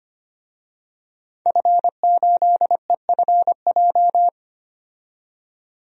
Morse code, heard as a keyed tone, about 25 wpm, 700 Hz. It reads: F8EFJ
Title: DXCC-100 Call Signs - 5 Characters - 25wpm